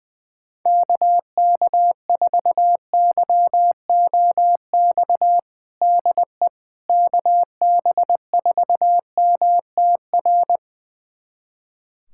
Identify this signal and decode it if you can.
Morse code, heard as a keyed tone, about 20 wpm, 700 Hz. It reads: KK4YOX DE KB4MTR
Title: USA Callsign DE Callsign 20wpm